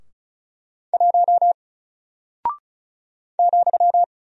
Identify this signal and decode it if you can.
Morse code, heard as a keyed tone, about 36 wpm, 700 Hz. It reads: 1 ,
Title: Single Character (Letter:Number:Punctuation:Prosign) Alt 35wpm